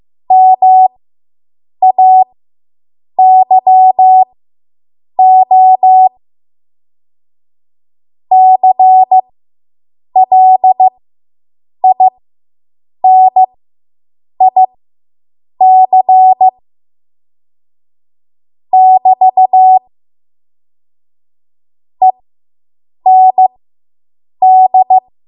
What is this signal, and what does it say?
Morse code, heard as a keyed tone, about 15 wpm, spaced as for 7 wpm, 750 Hz. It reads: MAYO CLINIC <BT> END